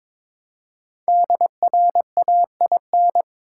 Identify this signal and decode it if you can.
Morse code, heard as a keyed tone, about 22 wpm, 700 Hz. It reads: DRAIN
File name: 5 Letter Words - First 3 Letters Given - 22wpm